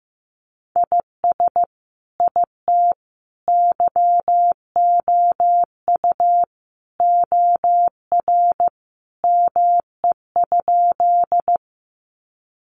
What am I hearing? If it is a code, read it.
Morse code, heard as a keyed tone, about 15 wpm, 700 Hz. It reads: IS IT YOU OR ME?
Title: Sentences from Top 200 Words 15wpm